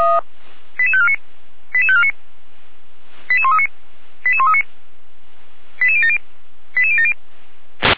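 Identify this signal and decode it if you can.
Five-tone selective call (ZVEI), heard as quick runs of five tones, tones of about 70 ms. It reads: 88549 88549 88139 88139 88070 88070
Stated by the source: Five-tone ZVEI selective call received on an SDR.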